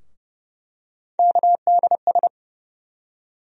Morse code, heard as a keyed tone, about 30 wpm, 700 Hz. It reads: KDS
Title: USA Call Sign Postfixes - 3 Letters 30wpm